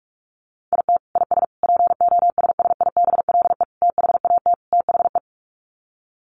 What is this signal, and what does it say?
Morse code, heard as a keyed tone, about 44 wpm, 700 Hz. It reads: IT IS POSSIBLE THAT THE
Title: Sets of 5 Words Encore2 45wpm